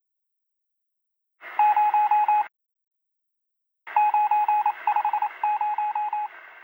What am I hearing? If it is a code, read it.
Morse code, heard as a keyed tone, about 28 wpm, 840 Hz. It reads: 0 950
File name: google-earth-vr-linux-vr-windows-gamerip-2016 / avahlschnc / 069. T1 J2s11 Earth Morse